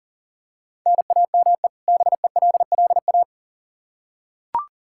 Morse code, heard as a keyed tone, about 40 wpm, 700 Hz. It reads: NAME BELLA
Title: Practice QSO - Exchange 1 - Beginning - 40wpm